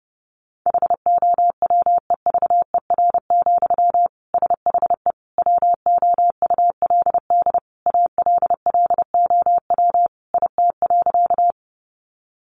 Morse code, heard as a keyed tone, about 30 wpm, 700 Hz. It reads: HOWEVER, SHE WOULD ALLOW IT.
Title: Sentences from Top 700 Words 30wpm